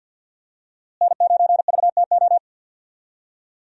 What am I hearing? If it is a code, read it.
Morse code, heard as a keyed tone, about 53 wpm, 690 Hz. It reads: N9VTO
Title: USA Call Signs - 5 Characters - Repeated 3x - 50wpm